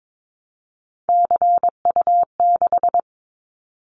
Morse code, heard as a keyed tone, about 22 wpm, 700 Hz. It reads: CU6
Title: Callsign Prefixes for Top 50 DXCC - 22wpm